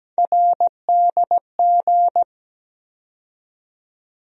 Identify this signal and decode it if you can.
Morse code, heard as a keyed tone, about 17 wpm, 700 Hz. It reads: RDG
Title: USA Call Sign Postfixes - 3 Letters 17wpm